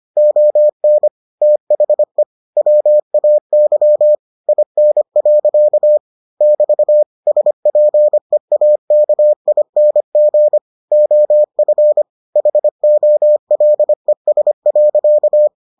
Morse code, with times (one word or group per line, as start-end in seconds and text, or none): 0.2-1.1 ON
1.4-2.2 THE
2.6-4.1 WAY
4.5-6.0 IN.
6.4-7.0 <BT>
7.3-10.6 SPEAKING
10.9-12.0 OF
12.4-15.5 HOLES.